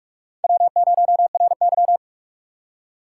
Morse code, heard as a keyed tone, about 44 wpm, 700 Hz. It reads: W0RY
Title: USA Call Signs - 4 Characters - 6x Character Spacing - 45wpm